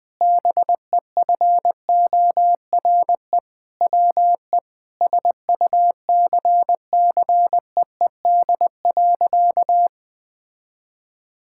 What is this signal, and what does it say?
Morse code, heard as a keyed tone, about 20 wpm, 700 Hz. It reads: BEFORE WE SUCCEED.
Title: Sentences with 7-Word Phrases - 20wpm